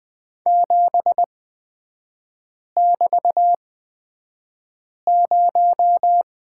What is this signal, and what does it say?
Morse code, heard as a keyed tone, about 20 wpm, spaced as for 5 wpm, 700 Hz. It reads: 7<BT>0
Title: 224 Lesson 36e - BT - Three Characters 20wpm